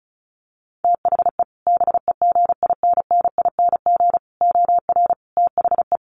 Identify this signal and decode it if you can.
Morse code, heard as a keyed tone, about 34 wpm, 700 Hz. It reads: THE BEGINNING OR THE
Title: Sentences from Top 800 Words 35wpm